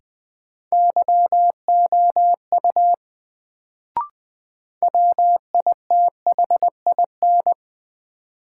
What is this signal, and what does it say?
Morse code, heard as a keyed tone, about 20 wpm, 700 Hz. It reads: YOU WITHIN
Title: Sets of 2 Words - New Word Drill - 20wpm